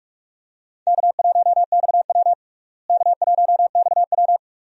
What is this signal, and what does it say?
Morse code, heard as a keyed tone, about 46 wpm, 700 Hz. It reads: K1XW K1XW
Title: USA Call Signs - 4 Characters - Repeated 2x - 45wpm